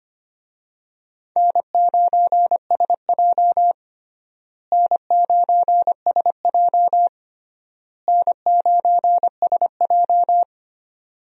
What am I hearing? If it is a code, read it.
Morse code, heard as a keyed tone, about 25 wpm, 700 Hz. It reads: N9SJ N9SJ N9SJ
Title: USA Call Signs - 4 Characters - Repeated 3x - 25wpm